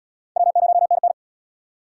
Morse code, heard as a keyed tone, about 76 wpm, 700 Hz. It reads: W0NN